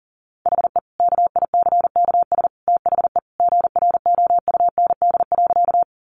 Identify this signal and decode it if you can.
Morse code, heard as a keyed tone, about 40 wpm, 700 Hz. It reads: HE KICKS THE GROUND.
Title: Additional Easy Phrases 40wpm